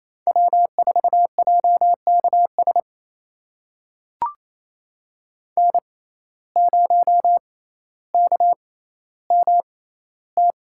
Morse code, heard as a keyed tone, about 28 wpm, spaced as for 14 wpm, 700 Hz. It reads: W4JKS N0KMT
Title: USA Call Signs - 5 Characters - 6x Character Spacing - 28wpm